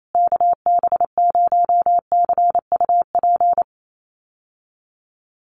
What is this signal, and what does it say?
Morse code, heard as a keyed tone, about 28 wpm, 700 Hz. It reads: KB0CUP
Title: USA Call Signs Repeated 3x 28wpm